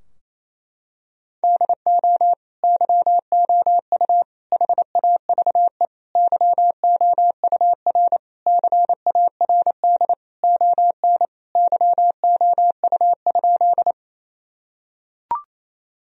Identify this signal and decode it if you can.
Morse code, heard as a keyed tone, about 27 wpm, 700 Hz. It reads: DO YOU HAVE YOUR CARD ON YOU?
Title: Sentences from Top 1000 Words - Review Part 4 - 28wpm